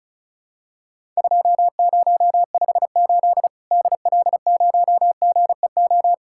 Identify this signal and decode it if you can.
Morse code, heard as a keyed tone, about 36 wpm, 700 Hz. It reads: 2058 DL0GEO
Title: Worked All Europe DX Contest - QTC Exchange - 35wpm